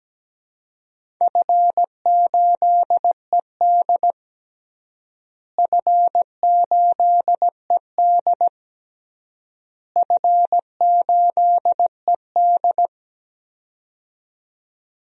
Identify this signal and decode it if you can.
Morse code, heard as a keyed tone, about 17 wpm, 700 Hz. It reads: F8ED F8ED F8ED
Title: DXCC-100 Call Signs - 4 Characters - Repeated 3x - 17wpm